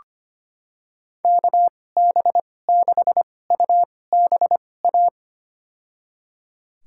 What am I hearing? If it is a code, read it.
Morse code, heard as a keyed tone, about 25 wpm, spaced as for 18 wpm, 700 Hz. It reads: KB6UBA
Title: USA Call Signs - 6 Characters - 2x Character Spacing - 25wpm